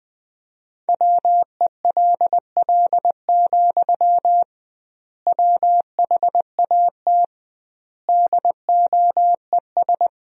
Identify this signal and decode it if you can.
Morse code, heard as a keyed tone, about 21 wpm, 700 Hz. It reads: WELL, WHAT DOES
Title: 2x Word-Spacing - Sentences from Top 500 Words - 20wpm